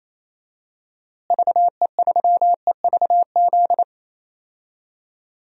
Morse code, heard as a keyed tone, about 28 wpm, 700 Hz. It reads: VE3EVZ
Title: DXCC-100 Call Signs - 6 Characters - Repeated 3x - 28wpm